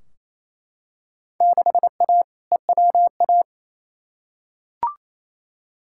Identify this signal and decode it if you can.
Morse code, heard as a keyed tone, about 28 wpm, 700 Hz. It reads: 6A EWA
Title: Field Day Exchanges Encore - 28wpm